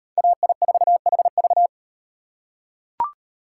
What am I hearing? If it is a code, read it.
Morse code, heard as a keyed tone, about 35 wpm, 700 Hz. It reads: AI4HV